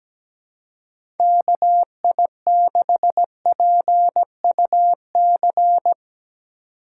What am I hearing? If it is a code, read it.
Morse code, heard as a keyed tone, about 17 wpm, 700 Hz. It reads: KI6PUC
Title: USA Call Signs - 6 Characters - 4x Character Spacing - 17wpm